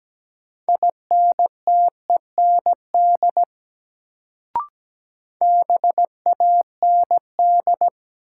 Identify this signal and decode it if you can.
Morse code, heard as a keyed tone, about 17 wpm, 700 Hz. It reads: INTEND BAND
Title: Sentences from Top 1600 Words - New Word Drill - 17wpm